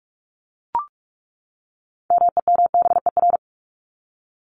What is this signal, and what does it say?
Morse code, heard as a keyed tone, about 44 wpm, 700 Hz. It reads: MEMBER
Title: ES:S Suffix - Top 100 Words with Base Words 45wpm